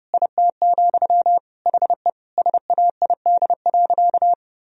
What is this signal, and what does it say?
Morse code, heard as a keyed tone, about 32 wpm, 700 Hz. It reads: IT, HE SAID.
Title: Sentences from Top 1000 Words - Review Part 1 - 30wpm